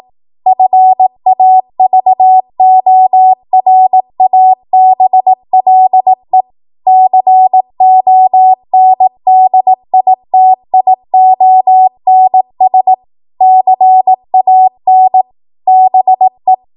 Morse code, heard as a keyed tone, about 18 wpm, 750 Hz. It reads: FAVORABLE CONDITIONS CAN BE
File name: Morse / 18WPM